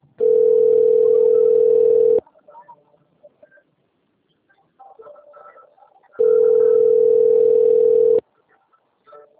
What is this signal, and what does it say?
Ringback tone (North America), 440 + 480 Hz.
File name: rg-1600-unknown- - - .42051